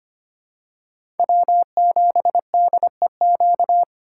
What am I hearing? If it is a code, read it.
Morse code, heard as a keyed tone, about 25 wpm, 700 Hz. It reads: W7DEQ
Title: USA Call Signs - 5 Characters - 4x Character Spacing - 25wpm